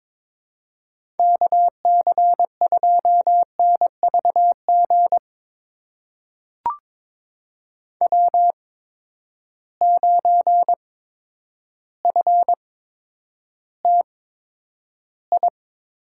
Morse code, heard as a keyed tone, about 22 wpm, spaced as for 10 wpm, 700 Hz. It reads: KC2NVG W9FTI
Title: USA Call Signs - 8x Character Spacing - 22wpm